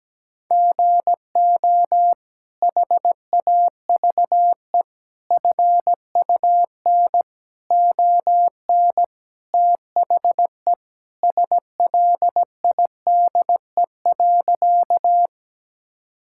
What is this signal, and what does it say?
Morse code, heard as a keyed tone, about 17 wpm, 700 Hz. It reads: GO HAVE FUN ON THE SLIDE.
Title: Sentences from Top 1600 Words 17wpm